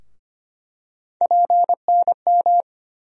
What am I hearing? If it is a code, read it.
Morse code, heard as a keyed tone, about 25 wpm, 700 Hz. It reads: PNM
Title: USA Call Sign Postfixes - 3 Letters 25wpm